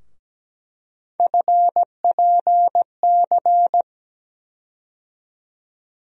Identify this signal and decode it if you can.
Morse code, heard as a keyed tone, about 17 wpm, 700 Hz. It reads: FPC
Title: USA Call Sign Postfixes - 3 Letters 17wpm